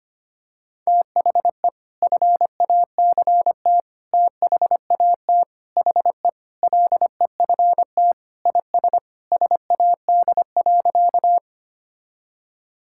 Morse code, heard as a keyed tone, about 25 wpm, 700 Hz. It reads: THE FACT THAT HE LEFT IS SAD.
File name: practice / 25wpm / Sentences with 2-Word Phrases - 25wpm